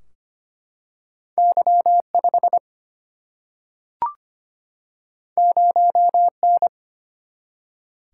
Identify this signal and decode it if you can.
Morse code, heard as a keyed tone, about 25 wpm, 700 Hz. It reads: Y 5 0 N
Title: Single-Letter Single-Number 25wpm